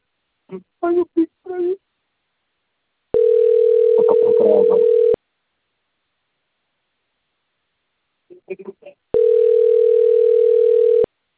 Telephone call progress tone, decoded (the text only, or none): ringback tone